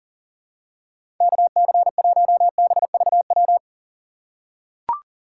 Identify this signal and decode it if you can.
Morse code, heard as a keyed tone, about 40 wpm, 700 Hz. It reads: KC1BVW
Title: USA Call Signs - 6 Characters - 4x Character Spacing - 40wpm